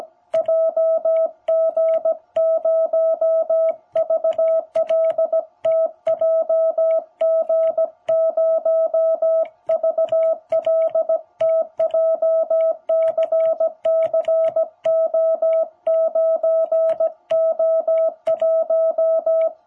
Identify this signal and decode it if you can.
Morse code, heard as a keyed tone, about 17 wpm, 660 Hz. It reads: JG0VLT JG0VLT JCCO9O1
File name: jh8cbh / 2021cw